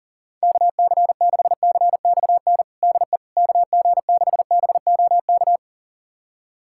Morse code, heard as a keyed tone, about 40 wpm, 700 Hz. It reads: KC6CXN DE KG6BOK
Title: USA Callsign DE Callsign - 6 Characters - 40wpm